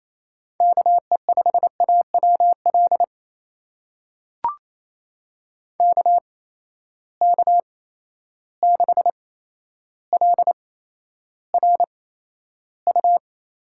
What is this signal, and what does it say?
Morse code, heard as a keyed tone, about 28 wpm, spaced as for 12 wpm, 700 Hz. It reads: KE5AWL KK6LRU
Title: USA Call Signs - 8x Character Spacing - 28wpm